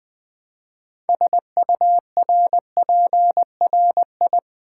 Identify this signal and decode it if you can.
Morse code, heard as a keyed tone, about 20 wpm, 700 Hz. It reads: SURPRI
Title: Build A Word - 3 Letters At A Time - 9-Letter Words - 20wpm